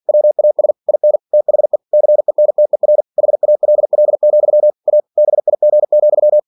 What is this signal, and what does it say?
Morse code, heard as a keyed tone, about 47 wpm, 600 Hz. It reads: WAS IN THE CENTER HALL, A BIG,
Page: Never 47wpm